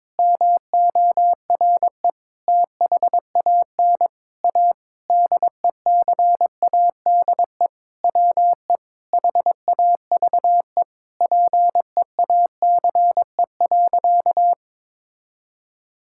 Morse code, heard as a keyed tone, about 22 wpm, 700 Hz. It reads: MORE THAN A DECADE WE HAVE PEACE.
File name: Sentences with 9-Word Phrases - 22wpm